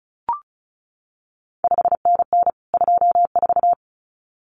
Morse code, heard as a keyed tone, about 36 wpm, 700 Hz. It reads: 5NN 24
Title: CQ WW DX Contest Exchange Encore - 35wpm